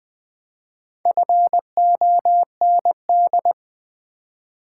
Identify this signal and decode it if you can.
Morse code, heard as a keyed tone, about 20 wpm, 700 Hz. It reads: FOND
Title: Build A Word - 1 Letter At A Time - 4-Letter Words - 20wpm